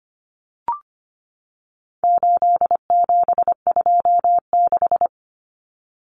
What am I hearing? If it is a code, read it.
Morse code, heard as a keyed tone, about 25 wpm, 700 Hz. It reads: 8726
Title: Common Cut Numbers - 4-Digit 25wpm